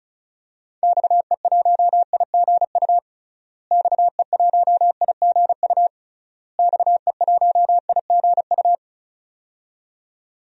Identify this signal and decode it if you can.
Morse code, heard as a keyed tone, about 34 wpm, 700 Hz. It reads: XE1IGU XE1IGU XE1IGU
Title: DXCC-100 Call Signs - 6 Characters - Repeated 3x - 35wpm